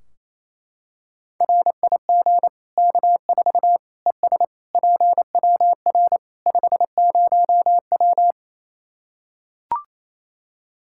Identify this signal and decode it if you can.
Morse code, heard as a keyed tone, about 27 wpm, 700 Hz. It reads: RIG K4 ES PWR 50W
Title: K1USN SST QSO Practice 28wpm